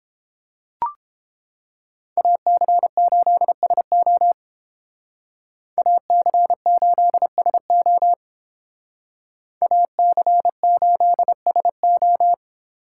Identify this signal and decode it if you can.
Morse code, heard as a keyed tone, about 28 wpm, 700 Hz. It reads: AC8SO AC8SO AC8SO